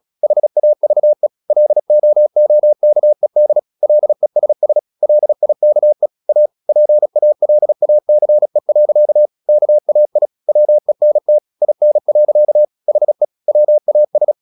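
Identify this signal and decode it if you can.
Morse code, heard as a keyed tone, about 36 wpm, 600 Hz. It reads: HAVE LOOKED LESS LIKE A PALACE. KAI WENT IN. HE WAS
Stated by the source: Morse practice at 36 wpm, text of Ken Follett's novel Never, from CW med Gnister.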